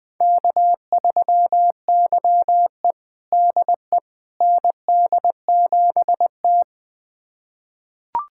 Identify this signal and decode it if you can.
Morse code, heard as a keyed tone, about 20 wpm, 700 Hz. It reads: K3YE DE ND7T
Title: USA Callsign DE Callsign - 4 Characters - 20wpm